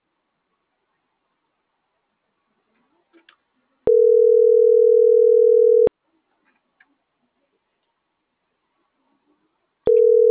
Ringback tone (North America), 440 + 480 Hz.